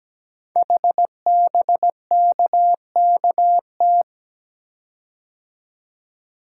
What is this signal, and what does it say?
Morse code, heard as a keyed tone, about 17 wpm, 700 Hz. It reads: HBKKT
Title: Code Groups - 5 Letters - 17wpm